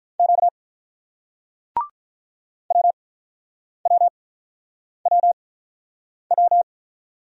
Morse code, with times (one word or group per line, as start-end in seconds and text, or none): 0.2-0.5 Q
2.7-6.6 WWWW